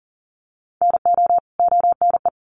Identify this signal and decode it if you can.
Morse code, heard as a keyed tone, about 40 wpm, 700 Hz. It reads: NO ONE
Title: Sets of 2 Words 40wpm